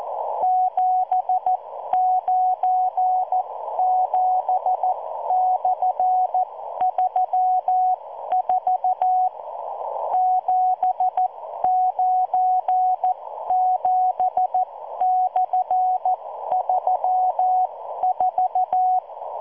Morse code, heard as a keyed tone, about 14 wpm, spaced as for 12 wpm, 740 Hz. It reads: 797/34 797/34